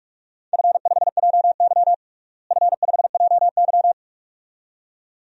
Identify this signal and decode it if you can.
Morse code, heard as a keyed tone, about 42 wpm, 700 Hz. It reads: F5JY F5JY
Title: DXCC-100 Call Signs - 4 Characters - Repeated 2x - 45wpm